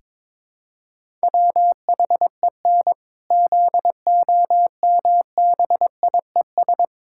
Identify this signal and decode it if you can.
Morse code, heard as a keyed tone, about 22 wpm, 700 Hz. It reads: WHEN ZOMBIES
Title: Pangram Challenge v2 - 22wpm